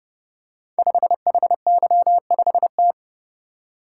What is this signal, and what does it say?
Morse code, heard as a keyed tone, about 30 wpm, 700 Hz. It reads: 5HY5T
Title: Code Groups - 5 Letter:Numbers - 30wpm